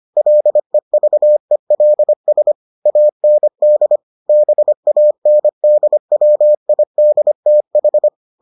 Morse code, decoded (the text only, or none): LEVELS AND BANDWIDTH